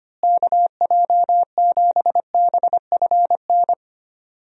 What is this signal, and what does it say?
Morse code, heard as a keyed tone, about 25 wpm, 700 Hz. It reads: KJ7BFN
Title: USA Call Signs - 6 Characters - Repeated 3x - 25wpm